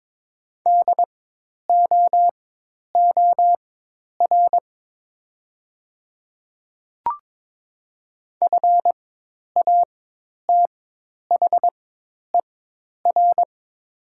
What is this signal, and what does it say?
Morse code, heard as a keyed tone, about 22 wpm, spaced as for 10 wpm, 700 Hz. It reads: DOOR FATHER
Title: 4x Character Spacing - Top 500 Words - 22wpm